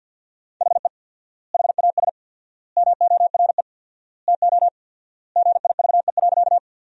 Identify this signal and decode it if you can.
Morse code, heard as a keyed tone, about 50 wpm, 700 Hz. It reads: HE HAS MORE TO GIVE.
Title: 4x Word-Spacing - Sentences from Top 200 Words - 50wpm